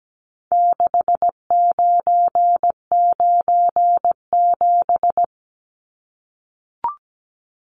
Morse code, heard as a keyed tone, about 17 wpm, 700 Hz. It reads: 6997
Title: Numbers - 4-Digit 17wpm